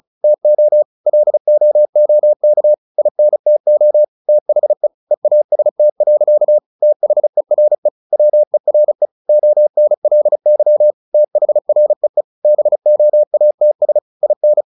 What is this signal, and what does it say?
Morse code, heard as a keyed tone, about 34 wpm, 600 Hz. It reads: TO LOOK INTO THE EAST. THERE WERE ONLY THREE BOATS IN
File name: the old man and the sea / the old man and the sea 1 35wpm